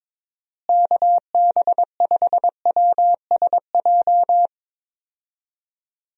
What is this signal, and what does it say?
Morse code, heard as a keyed tone, about 22 wpm, 700 Hz. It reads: KB5WSJ
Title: USA Call Signs - 6 Characters - Repeated 3x - 22wpm